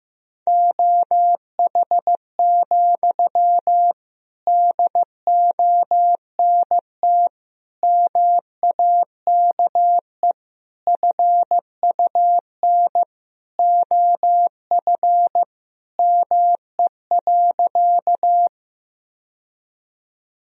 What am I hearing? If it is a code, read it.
Morse code, heard as a keyed tone, about 15 wpm, 700 Hz. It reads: OH, DONT MAKE FUN OF ME.
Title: Sentences from Top 1000 Words - Review Part 4 - 15wpm